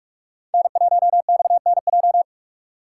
Morse code, heard as a keyed tone, about 46 wpm, 700 Hz. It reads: N1XNJ
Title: USA Call Signs - 4x Character Spacing - 45wpm